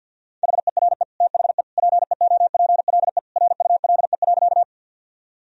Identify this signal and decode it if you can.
Morse code, heard as a keyed tone, about 53 wpm, 700 Hz. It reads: HERE THE PEOPLE RULE.
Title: Sentences from Top 1000 Words - Review Part 2 - 50wpm